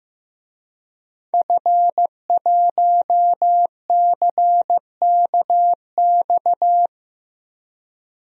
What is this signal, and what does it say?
Morse code, heard as a keyed tone, about 15 wpm, 700 Hz. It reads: F1CKX
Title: DXCC-100 Call Signs - 5 Characters - Repeated 2x - 15wpm